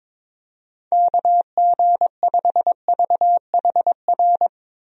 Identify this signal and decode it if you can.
Morse code, heard as a keyed tone, about 21 wpm, 700 Hz. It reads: KG5VHR
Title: USA Call Signs - 6 Characters - 22wpm